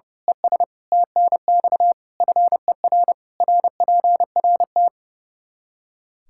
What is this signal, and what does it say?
Morse code, heard as a keyed tone, about 32 wpm, 700 Hz. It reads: ES TNX FER RPRT
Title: Practice QSO - Exchange 2 - Info - 30wpm